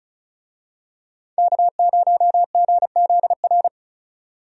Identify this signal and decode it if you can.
Morse code, heard as a keyed tone, about 36 wpm, 700 Hz. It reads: K0GZR
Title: USA Call Signs - 5 Characters - 35wpm